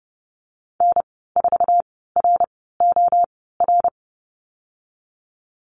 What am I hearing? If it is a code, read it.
Morse code, heard as a keyed tone, about 30 wpm, spaced as for 17 wpm, 700 Hz. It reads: N4ROR
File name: Lesson 15g - Number 5 - Call Signs 30wpm